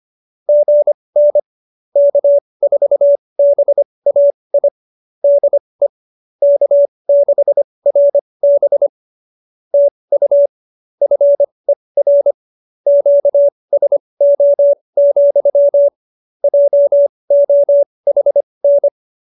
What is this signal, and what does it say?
Morse code, heard as a keyed tone, about 25 wpm, spaced as for 20 wpm, 570 Hz. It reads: GN K4BAI DE K6RB TU FER QSO, JOHN
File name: QSO301 20